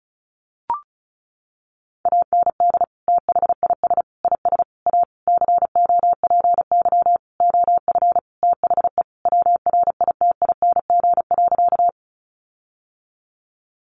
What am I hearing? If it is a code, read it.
Morse code, heard as a keyed tone, about 34 wpm, 700 Hz. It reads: AND THIS IS A COPY OF THE WRITING.
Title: Sentences from Top 1400 Words 35wpm